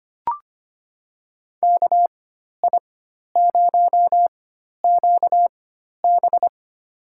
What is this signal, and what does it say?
Morse code, heard as a keyed tone, about 25 wpm, spaced as for 12 wpm, 700 Hz. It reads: KI0QB
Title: USA Call Signs - 5 Characters - 4x Character Spacing - 25wpm